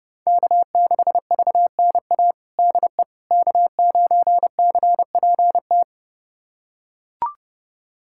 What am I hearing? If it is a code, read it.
Morse code, heard as a keyed tone, about 32 wpm, 700 Hz. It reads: K6VNA DE K9CPT
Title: USA Callsign DE Callsign - 5 Characters - 30wpm